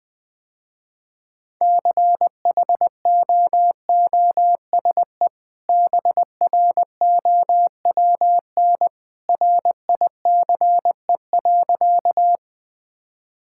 Morse code, heard as a keyed tone, about 20 wpm, 700 Hz. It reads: CHOOSE BROWN RICE.
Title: Easy Sentences with Hints - 20wpm